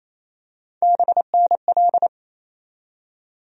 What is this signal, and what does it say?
Morse code, heard as a keyed tone, about 28 wpm, 700 Hz. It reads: BNL
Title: USA Call Sign Postfixes - 3 Letters 28wpm